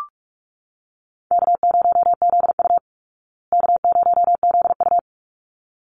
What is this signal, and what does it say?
Morse code, heard as a keyed tone, about 44 wpm, 700 Hz. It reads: K0ZU K0ZU
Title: Easy Call Signs Repeated 2x - 45wpm